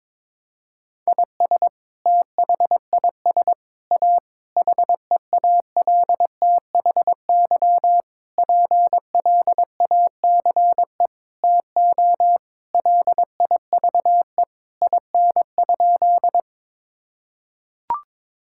Morse code, heard as a keyed tone, about 21 wpm, 700 Hz. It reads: IS THIS A HEALTHY PLACE TO LIVE IN?
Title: Sentences from Top 1000 Words - Review Part 5 - 22wpm